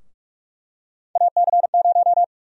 Morse code, heard as a keyed tone, about 47 wpm, 700 Hz. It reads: AC0
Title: USA Call Sign Prefixes - Repeated 2x - 45wpm